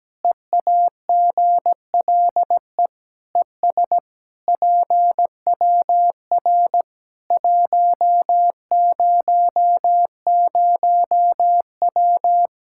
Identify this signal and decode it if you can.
Morse code, heard as a keyed tone, about 17 wpm, 700 Hz. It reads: EAGLE ES PWR 100W
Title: Practice QSO - Exchange 2 - Info - 17wpm